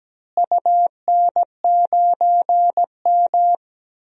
Morse code, heard as a keyed tone, about 17 wpm, 700 Hz. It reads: UN9M
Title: DXCC-100 Call Signs - 4 Characters - Repeated 2x - 17wpm